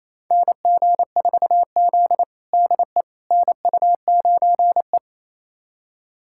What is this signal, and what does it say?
Morse code, heard as a keyed tone, about 28 wpm, 700 Hz. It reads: NG4Z DE NU9E
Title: USA Callsign DE Callsign - 4 Characters - 28wpm